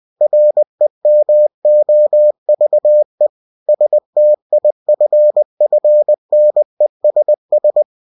Morse code, heard as a keyed tone, about 20 wpm, 600 Hz. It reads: REMOVE STIFFNESS
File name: cw-books / winnie-the-pooh / winnie-20wpm-page-0022